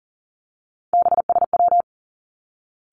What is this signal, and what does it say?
Morse code, heard as a keyed tone, about 42 wpm, 700 Hz. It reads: BSW
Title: Three Challenging Letter-Number Combinations 40wpm